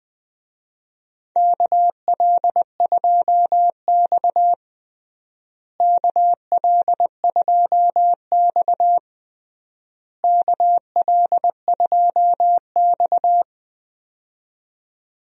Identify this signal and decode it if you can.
Morse code, heard as a keyed tone, about 20 wpm, 700 Hz. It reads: KL2X KL2X KL2X
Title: USA Call Signs - 4 Characters - Repeated 3x - 20wpm